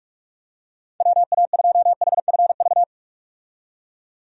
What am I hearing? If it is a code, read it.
Morse code, heard as a keyed tone, about 43 wpm, 700 Hz. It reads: WA2HFV
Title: USA Call Signs - 6 Characters - 45wpm